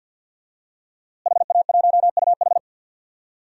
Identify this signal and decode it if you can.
Morse code, heard as a keyed tone, about 48 wpm, 700 Hz. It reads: HA1UH
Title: DXCC-100 Call Signs - 5 Characters - Repeated 3x - 50wpm